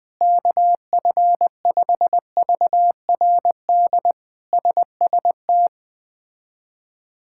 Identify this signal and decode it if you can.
Morse code, heard as a keyed tone, about 21 wpm, 700 Hz. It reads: KF5VRD SST
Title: K1USN SST QSO Practice - 20wpm